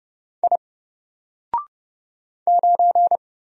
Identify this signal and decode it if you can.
Morse code, heard as a keyed tone, about 30 wpm, 700 Hz. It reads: I 9
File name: Single-Letter-Number-30wpm